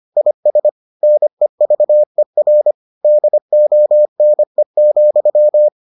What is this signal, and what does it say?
Morse code, heard as a keyed tone, about 25 wpm, 600 Hz. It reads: IS NEVER DONE,